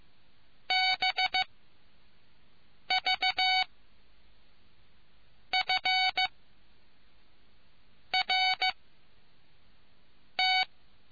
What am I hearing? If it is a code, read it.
Morse code, heard as a keyed tone, about 15 wpm, spaced as for 4 wpm, 750 Hz. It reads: BVFRT